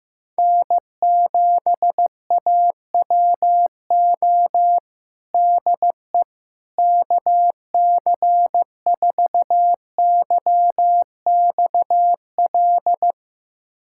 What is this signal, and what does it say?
Morse code, heard as a keyed tone, about 15 wpm, 700 Hz. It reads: N7AWO DE KC4YXL
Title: USA Callsign DE Callsign 15wpm